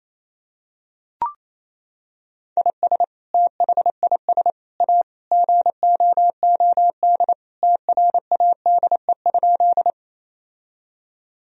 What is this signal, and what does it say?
Morse code, heard as a keyed tone, about 28 wpm, 700 Hz. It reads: IS THIS A GOOD TRADE?
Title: Sentences from Top 1000 Words - Review Part 4 - 28wpm